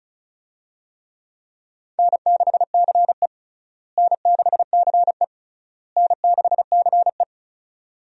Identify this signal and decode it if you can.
Morse code, heard as a keyed tone, about 36 wpm, 700 Hz. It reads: N6CE N6CE N6CE
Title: USA Call Signs - 4 Characters - Repeated 3x - 35wpm